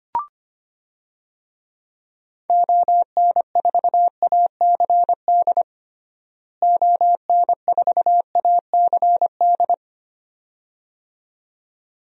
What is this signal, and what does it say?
Morse code, heard as a keyed tone, about 25 wpm, 700 Hz. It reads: ON4ACD ON4ACD
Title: DXCC-100 Call Signs - 6 Characters - Repeated 2x - 25wpm